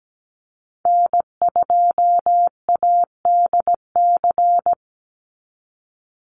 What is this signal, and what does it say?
Morse code, heard as a keyed tone, about 17 wpm, 700 Hz. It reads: N2ADC
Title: Call Signs Repeated 3x - 17wpm